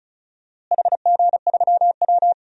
Morse code, heard as a keyed tone, about 34 wpm, 700 Hz. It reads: HG3W
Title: DXCC-100 Call Signs - 4 Characters - Repeated 3x - 35wpm